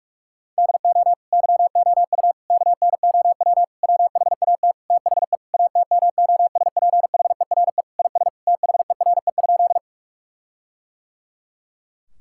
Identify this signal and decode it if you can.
Morse code, heard as a keyed tone, about 46 wpm, 700 Hz. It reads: DO YOU KNOW WHAT THE ATMOSPHERE IS THERE?
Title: Sentences from Top 1600 Words 45wpm